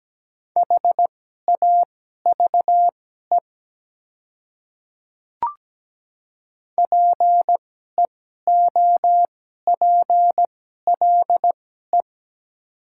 Morse code, heard as a keyed tone, about 17 wpm, spaced as for 12 wpm, 700 Hz. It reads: HAVE PEOPLE
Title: 2x Character Spacing - Top 100 Words - 17wpm